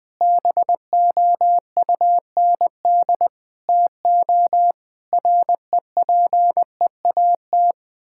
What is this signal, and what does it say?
Morse code, heard as a keyed tone, about 20 wpm, 700 Hz. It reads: BOUND TO REPEAT
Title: Sentences with 8-Word Phrases - 20wpm